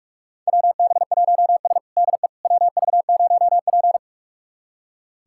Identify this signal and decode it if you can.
Morse code, heard as a keyed tone, about 46 wpm, 700 Hz. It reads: WB1S DE WV0P
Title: USA Callsign DE Callsign - 4 Characters - 45wpm